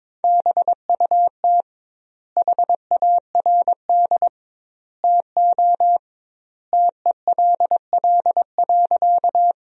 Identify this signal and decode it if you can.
Morse code, heard as a keyed tone, about 22 wpm, 700 Hz. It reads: BUT HARD TO TELL.
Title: 2x Word-Spacing - Sentences from Top 300 Words - 22wpm